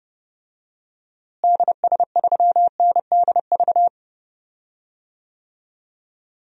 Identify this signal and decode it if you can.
Morse code, heard as a keyed tone, about 30 wpm, 700 Hz. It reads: DS3NDV
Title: DXCC-100 Call Signs - 6 Characters - Repeated 2x - 30wpm